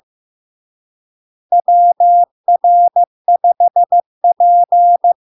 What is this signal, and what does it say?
Morse code, heard as a keyed tone, about 15 wpm, 700 Hz. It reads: WR5P
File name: LARK-FD-2022-Log-in-CW-15wpm-700-Hz